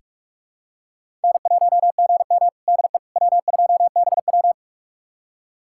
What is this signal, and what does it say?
Morse code, heard as a keyed tone, about 46 wpm, 700 Hz. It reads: N1GM DE W2BW
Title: USA Callsign DE Callsign - 4 Characters - 45wpm